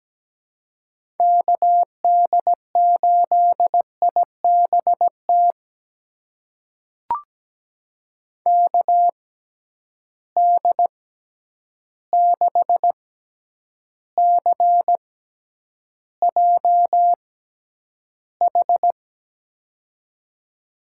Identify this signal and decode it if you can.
Morse code, heard as a keyed tone, about 17 wpm, spaced as for 9 wpm, 700 Hz. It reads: KD8IBT KD6CJH
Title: USA Call Signs - 6 Characters - 6x Character Spacing - 17wpm